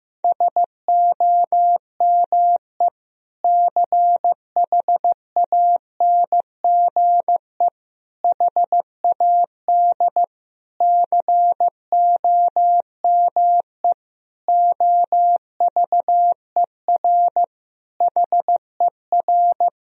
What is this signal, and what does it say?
Morse code, heard as a keyed tone, about 16 wpm, 700 Hz. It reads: SOME CHANGE HAD COME OVER HER
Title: Sentences from Top 1000 Words - Review Part 2 - 15wpm